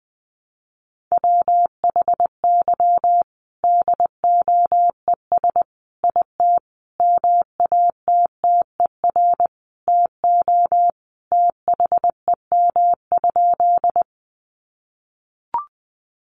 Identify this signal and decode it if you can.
Morse code, heard as a keyed tone, about 20 wpm, 700 Hz. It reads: WHY DOES IT MATTER TO THEM?
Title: Sentences from Top 700 Words 20wpm